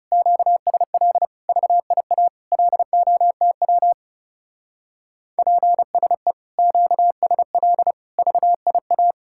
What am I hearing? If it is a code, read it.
Morse code, heard as a keyed tone, about 37 wpm, 700 Hz. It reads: QSL VIA LOTW PSE QSL VIA
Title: Speed-Racing Common QSO Phrases - 30wpm